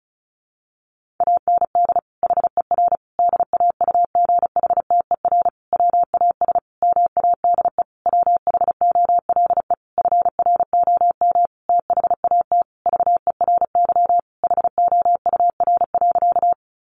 Morse code, heard as a keyed tone, about 34 wpm, 700 Hz. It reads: AND HER DAUGHTER WAS MADE WHOLE FROM THAT VERY HOUR.
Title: Sentences from Top 1500 Words 35wpm